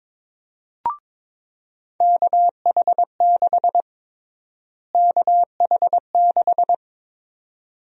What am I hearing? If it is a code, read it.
Morse code, heard as a keyed tone, about 21 wpm, 700 Hz. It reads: KH6 KH6
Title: USA Call Sign Prefixes - Repeated 2x - 22wpm